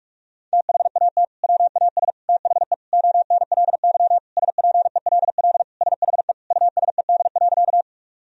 Morse code, heard as a keyed tone, about 46 wpm, 700 Hz. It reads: THAT WAS THE ONLY SPELL SHE USED.
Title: Sentences from Top 1000 Words - Review Part 1 - 45wpm